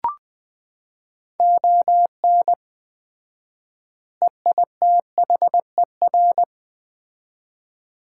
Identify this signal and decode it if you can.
Morse code, heard as a keyed tone, about 20 wpm, 700 Hz. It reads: ON EITHER
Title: 4x Word-Spacing - Sets of 5 Words - 20wpm